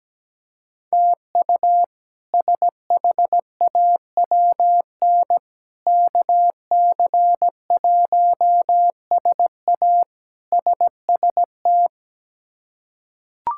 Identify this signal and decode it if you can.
Morse code, heard as a keyed tone, about 17 wpm, 700 Hz. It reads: TU SHAWN KC1SA SST
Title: K1USN SST QSO Practice - 17wpm